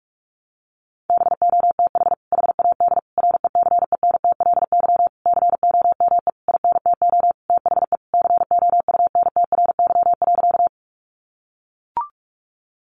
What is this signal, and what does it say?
Morse code, heard as a keyed tone, about 44 wpm, 700 Hz. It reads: BOTH HAD RECENTLY COME INTO THE COUNTRY.
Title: Sentences from Top 1200 Words 45wpm